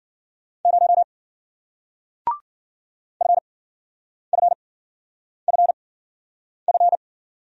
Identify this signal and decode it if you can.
Morse code, heard as a keyed tone, about 54 wpm, spaced as for 9 wpm, 700 Hz. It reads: 0 FFFF